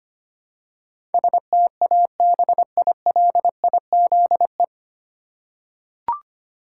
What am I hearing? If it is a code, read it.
Morse code, heard as a keyed tone, about 25 wpm, 700 Hz. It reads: STABILIZE
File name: Nine Letter Words - 25wpm